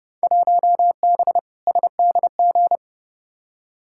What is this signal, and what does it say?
Morse code, heard as a keyed tone, about 29 wpm, 700 Hz. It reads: 1B SDG
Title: Field Day Exchanges - 30wpm